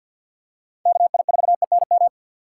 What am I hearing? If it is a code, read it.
Morse code, heard as a keyed tone, about 47 wpm, 700 Hz. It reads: KI4ENM
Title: USA Call Signs - 6 Characters - 4x Character Spacing - 50wpm